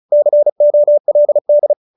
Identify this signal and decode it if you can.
Morse code, heard as a keyed tone, about 34 wpm, 590 Hz. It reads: COLD